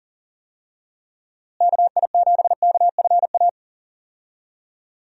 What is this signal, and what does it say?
Morse code, heard as a keyed tone, about 40 wpm, 700 Hz. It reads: KI7KFA
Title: USA Call Signs - 6 Characters - Repeated 3x - 40wpm